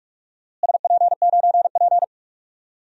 Morse code, heard as a keyed tone, about 46 wpm, 700 Hz. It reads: SP9P
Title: DXCC-100 Call Signs - 4 Characters - Repeated 2x - 45wpm